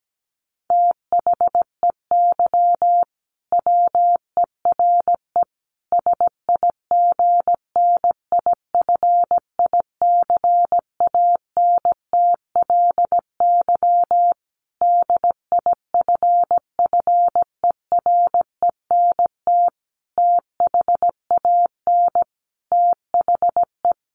Morse code, heard as a keyed tone, about 17 wpm, 700 Hz. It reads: THEY WERE SIGNIFICANTLY DIFFERENT THAN THE
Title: Sentences from Top 1500 Words 17wpm